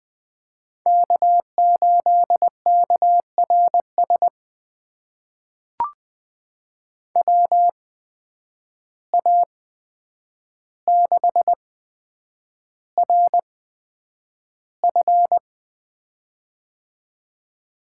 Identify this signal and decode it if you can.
Morse code, heard as a keyed tone, about 20 wpm, spaced as for 9 wpm, 700 Hz. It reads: K8KRS WA6RF
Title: USA Call Signs - 5 Characters - 8x Character Spacing - 20wpm